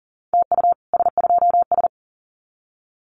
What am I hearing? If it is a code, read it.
Morse code, heard as a keyed tone, about 40 wpm, 700 Hz. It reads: TU S2S
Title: Common QSO Phrases Encore 40wpm